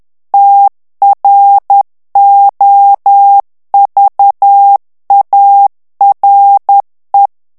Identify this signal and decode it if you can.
Morse code, heard as a keyed tone, about 10 wpm, 800 Hz. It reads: TROVARE
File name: morse mp3 / ITALIEN texte clair 1 vitesse 10 mots minute